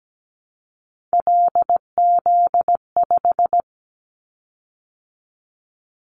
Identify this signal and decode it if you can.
Morse code, heard as a keyed tone, about 17 wpm, 700 Hz. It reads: LZ5
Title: Callsign Prefixes for Top 100 DXCC - 17wpm